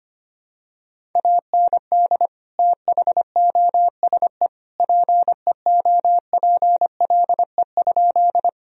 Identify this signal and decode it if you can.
Morse code, heard as a keyed tone, about 25 wpm, 700 Hz. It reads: AND THOSE PEOPLE?
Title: Sentences from Top 1000 Words - Review Part 1 - 25wpm